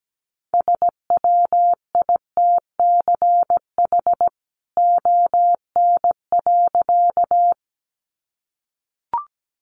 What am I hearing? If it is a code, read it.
Morse code, heard as a keyed tone, about 17 wpm, 700 Hz. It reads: SWITCH ON.
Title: A Sound Of Thunder 17wpm